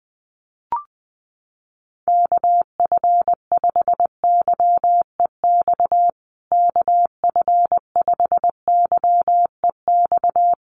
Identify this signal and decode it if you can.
Morse code, heard as a keyed tone, about 20 wpm, 700 Hz. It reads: KF5YEX KF5YEX
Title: USA Call Signs Repeated 3x 20wpm